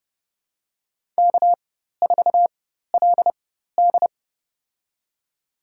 Morse code, heard as a keyed tone, about 31 wpm, spaced as for 14 wpm, 700 Hz. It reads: K4LD
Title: USA Call Signs - 4 Characters - 4x Character Spacing - 30wpm